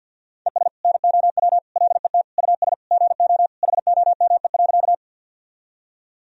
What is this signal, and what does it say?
Morse code, heard as a keyed tone, about 50 wpm, 700 Hz. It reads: ES NOW LET US GO HOME.
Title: Sentences from Top 300 Words and Abbreviations 50wpm